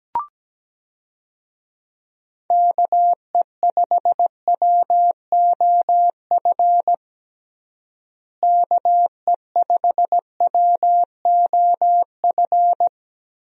Morse code, heard as a keyed tone, about 17 wpm, 700 Hz. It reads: KE5WOF KE5WOF
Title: USA Call Signs - 6 Characters - Repeated 3x - 17wpm